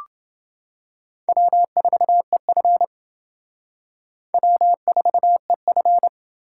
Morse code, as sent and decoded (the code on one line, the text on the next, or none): .-- ....- . ..-. / .-- ....- . ..-.
W4EF W4EF